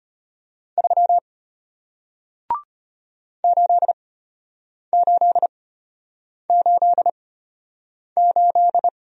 Morse code, heard as a keyed tone, about 34 wpm, spaced as for 8 wpm, 700 Hz. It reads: 3 8888